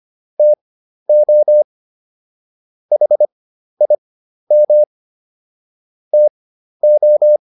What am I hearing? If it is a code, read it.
Morse code, heard as a keyed tone, about 25 wpm, spaced as for 12 wpm, 600 Hz. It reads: TO HIM TO